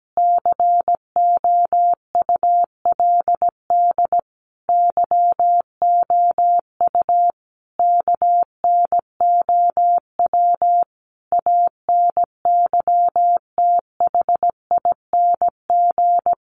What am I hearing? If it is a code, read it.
Morse code, heard as a keyed tone, about 17 wpm, 700 Hz. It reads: COULD YOU KNOW ANYTHING
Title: Sentences from Top 1000 Words 17wpm